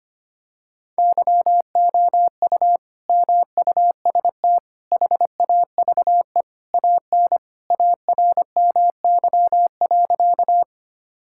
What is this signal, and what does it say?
Morse code, heard as a keyed tone, about 25 wpm, 700 Hz. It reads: YOU MUST HAVE AN ARMY.
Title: Sentences from Top 1000 Words - Review Part 5 - 25wpm